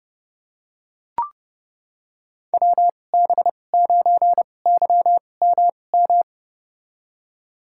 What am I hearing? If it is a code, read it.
Morse code, heard as a keyed tone, about 32 wpm, spaced as for 22 wpm, 700 Hz. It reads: WB9YMM
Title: USA Call Signs - 6 Characters - 2x Character Spacing - 30wpm